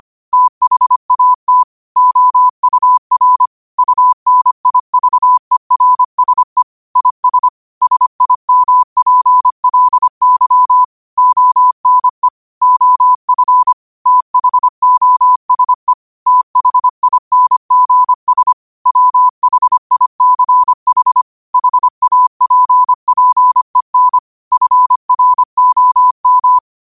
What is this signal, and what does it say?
Morse code, heard as a keyed tone, about 25 wpm, 1000 Hz. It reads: THAT OUR UNIVERSE IS SIMPLY ONE OF THOSE THINGS WHICH HAPPEN FROM
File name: qotdfiles / 25wpm